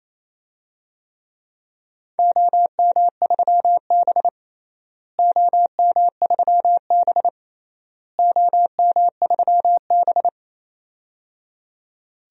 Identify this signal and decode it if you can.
Morse code, heard as a keyed tone, about 28 wpm, 700 Hz. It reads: OM3B OM3B OM3B
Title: DXCC-100 Call Signs - 4 Characters - Repeated 3x - 28wpm